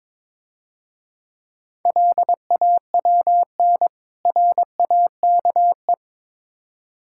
Morse code, heard as a keyed tone, about 25 wpm, 700 Hz. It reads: LAWN RAKE
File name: Easy Words - Speed-Racing with Hints - 22wpm